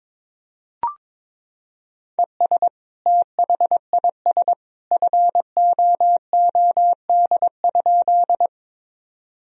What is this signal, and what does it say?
Morse code, heard as a keyed tone, about 22 wpm, 700 Hz. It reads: ES THIS FOOD?
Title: Sentences from Top 300 Words and Abbreviations 22wpm